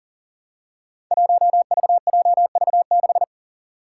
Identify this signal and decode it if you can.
Morse code, heard as a keyed tone, about 42 wpm, 700 Hz. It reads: 1VJV6
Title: Five Challenging Letter-Number Combinations 40wpm